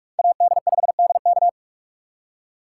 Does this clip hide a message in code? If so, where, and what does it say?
Yes: Morse code from 0.2 to 1.5 s, heard as a keyed tone, about 57 wpm, 700 Hz. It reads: AD5DK